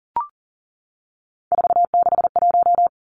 Morse code, heard as a keyed tone, about 40 wpm, 700 Hz. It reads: 461
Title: Numbers - 3-Digit 40wpm